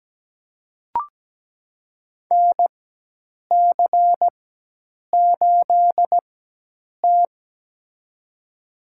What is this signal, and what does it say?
Morse code, heard as a keyed tone, about 17 wpm, spaced as for 8 wpm, 700 Hz. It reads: NC8T
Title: USA Call Signs - 4 Characters - 4x Character Spacing - 17wpm